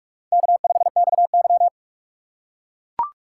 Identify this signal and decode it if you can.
Morse code, heard as a keyed tone, about 44 wpm, 700 Hz. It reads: K5XY
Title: USA Call Signs - 4 Characters - 6x Character Spacing - 45wpm